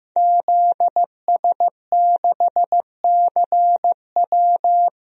Morse code, heard as a keyed tone, about 15 wpm, 700 Hz. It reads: ZS6CW
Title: DXCC-100 Call Signs - 5 Characters - Repeated 2x - 15wpm